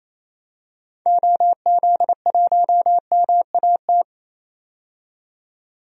Morse code, heard as a keyed tone, about 28 wpm, 700 Hz. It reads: OZ1MAT
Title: DXCC-100 Call Signs - 6 Characters - 28wpm